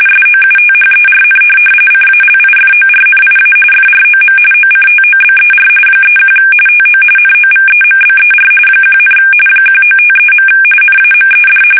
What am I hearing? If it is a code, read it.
RTTY, about 45.45 baud. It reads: SOUTHERN CALIFORNIA FROM OCTOBER 31 TO NOVEMBER 1, 1953. NAMED
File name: fsk850 / teletype-RTTY